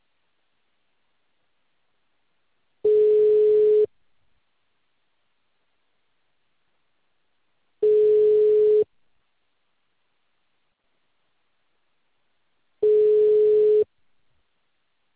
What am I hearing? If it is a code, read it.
Ringback tone (Europe, 425 Hz).
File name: out- -2044- - - .20168